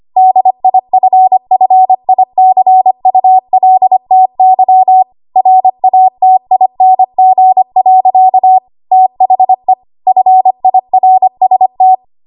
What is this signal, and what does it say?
Morse code, heard as a keyed tone, about 25 wpm, 750 Hz. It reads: DIFFICULTY RATING. THE FIRST
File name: Archive / 25 WPM / 25WPM